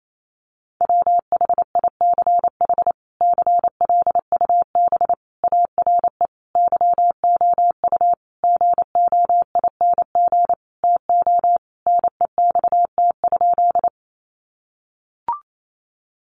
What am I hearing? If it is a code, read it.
Morse code, heard as a keyed tone, about 28 wpm, 700 Hz. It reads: WHICH CLUB ARE YOU GOING TO NEXT?
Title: Sentences from Top 1200 Words 28wpm